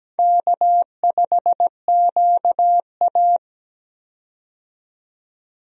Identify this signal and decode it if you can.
Morse code, heard as a keyed tone, about 17 wpm, 700 Hz. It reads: K5QA
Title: USA Call Signs - 4 Characters - 17wpm